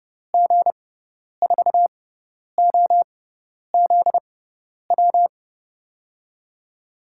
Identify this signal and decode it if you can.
Morse code, heard as a keyed tone, about 30 wpm, spaced as for 10 wpm, 700 Hz. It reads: G4OZW
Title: USA Call Signs - 6 Characters - 6x Character Spacing - 30wpm